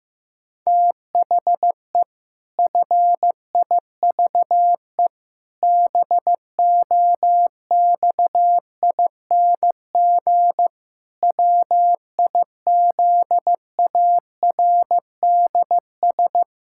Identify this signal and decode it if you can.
Morse code, heard as a keyed tone, about 15 wpm, 700 Hz. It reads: THE FIVE BOXING WIZARDS
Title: Pangram Challenge v2 - 15wpm